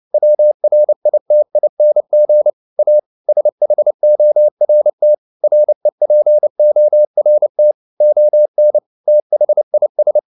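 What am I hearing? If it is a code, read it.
Morse code, heard as a keyed tone, about 29 wpm, 600 Hz. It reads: WRITING A SHORT REPORT ON THIS
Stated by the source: Morse code at 29 wpm, sending Ken Follett's novel Never in English, from CW med Gnister.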